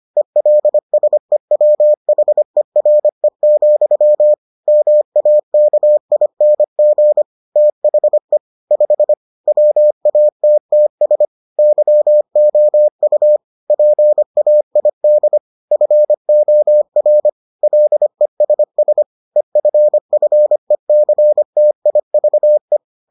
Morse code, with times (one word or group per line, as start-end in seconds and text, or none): 0.2-4.3 ELSEWHERE,
4.7-7.2 MAKING
7.6-8.4 THE
8.7-9.1 5
9.5-11.3 WATTS
11.6-13.4 YOU
13.7-15.4 PAID
15.7-17.3 FOR
17.6-19.0 LESS
19.4-22.8 EFFECTIVE